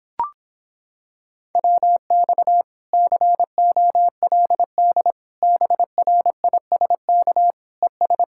Morse code, heard as a keyed tone, about 26 wpm, 700 Hz. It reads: WX COLD BRISK ES